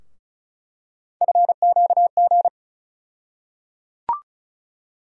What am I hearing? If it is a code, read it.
Morse code, heard as a keyed tone, about 36 wpm, 700 Hz. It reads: FQG
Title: USA Call Sign Postfixes - 3 Letters 35wpm